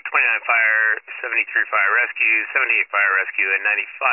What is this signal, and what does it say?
Emergency Alert System (SAME) tones, heard as harsh buzzing bursts, with speech. It is heard from 0.5 to 3.3 s.